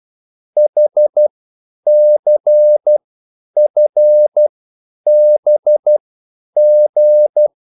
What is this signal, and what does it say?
Morse code, heard as a keyed tone, about 12 wpm, spaced as for 9 wpm, 600 Hz. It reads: HCFBG